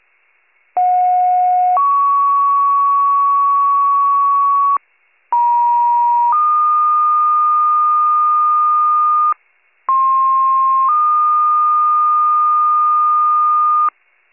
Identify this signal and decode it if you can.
Two-tone page, heard as a one-second tone, then a three-second tone, about 730 then 1120 Hz, about 950 then 1220 Hz, about 1060 then 1220 Hz. It is heard from 0.8 to 13.9 s.